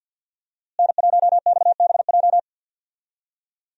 Morse code, heard as a keyed tone, about 52 wpm, 700 Hz. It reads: N1XBJ
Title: USA Call Signs - 5 Characters - Repeated 2x - 50wpm